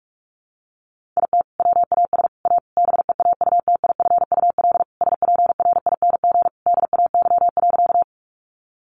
Morse code, heard as a keyed tone, about 44 wpm, 700 Hz. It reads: IT WAS A BEAUTIFUL SPRING DAY.
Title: Sentences from Top 1200 Words 45wpm